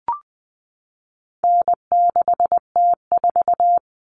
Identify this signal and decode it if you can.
Morse code, heard as a keyed tone, about 20 wpm, 700 Hz. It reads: N6T4
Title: All Cut Numbers - 4-Digit 20wpm